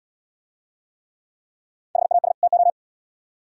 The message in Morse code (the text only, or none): HAS TO